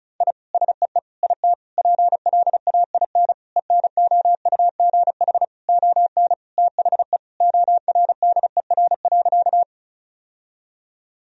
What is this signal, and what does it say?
Morse code, heard as a keyed tone, about 34 wpm, 700 Hz. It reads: I SEE IT PLAIN ENOUGH ON THE ORDER.
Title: Sentences from Top 1000 Words - Review Part 2 - 35wpm